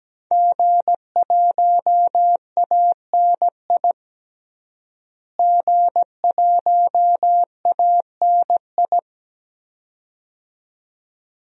Morse code, heard as a keyed tone, about 17 wpm, 700 Hz. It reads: G1ANI G1ANI
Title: DXCC-100 Call Signs - 5 Characters - Repeated 2x - 17wpm